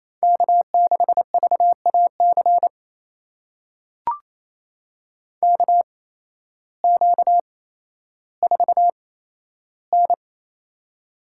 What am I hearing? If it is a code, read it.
Morse code, heard as a keyed tone, about 28 wpm, spaced as for 13 wpm, 700 Hz. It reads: K6VAC KQ4N
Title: USA Call Signs - 8x Character Spacing - 28wpm